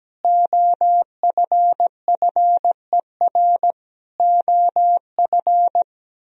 Morse code, heard as a keyed tone, about 17 wpm, 700 Hz. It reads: OFFER OF
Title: Sentences from Top 1000 Words - Review Part 4 - 17wpm